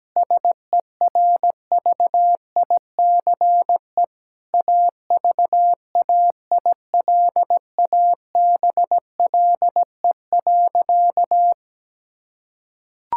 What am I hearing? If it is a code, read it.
Morse code, heard as a keyed tone, about 17 wpm, 700 Hz. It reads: SERVICE AVAILABLE.
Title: Sentences from Top 1000 Words - Review Part 6 - 17wpm